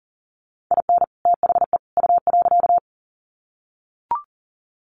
Morse code, heard as a keyed tone, about 40 wpm, 700 Hz. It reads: IN THE U.
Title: Sets of 3 Words 40wpm